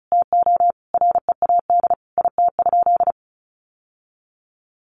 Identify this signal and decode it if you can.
Morse code, heard as a keyed tone, about 34 wpm, 700 Hz. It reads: TO READ IT?
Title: Sentences from Top 1200 Words 35wpm